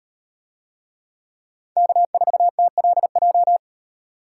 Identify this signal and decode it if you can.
Morse code, heard as a keyed tone, about 38 wpm, 700 Hz. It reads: K4TLJ